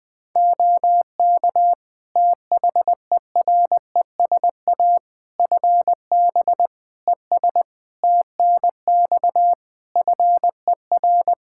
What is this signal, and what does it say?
Morse code, heard as a keyed tone, about 20 wpm, 700 Hz. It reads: OK THERESA FB ES TNX FER
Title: Practice QSO - Exchange 2 - Info - 20wpm